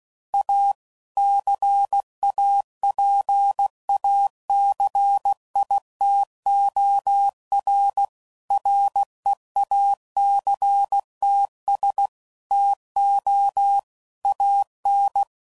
Morse code, heard as a keyed tone, about 16 wpm, 800 Hz. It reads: A CAPACITOR REACTS TO AN
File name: CW Code Groups @15WPM / Technical Subjects / XC